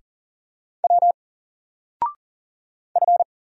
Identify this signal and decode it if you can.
Morse code, heard as a keyed tone, about 40 wpm, 700 Hz. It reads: W F
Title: Single Letter-Number 40wpm